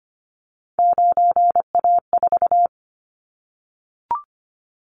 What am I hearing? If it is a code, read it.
Morse code, heard as a keyed tone, about 25 wpm, 700 Hz. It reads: 9A4
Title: Callsign Prefixes for Top 25 DXCC - 25wpm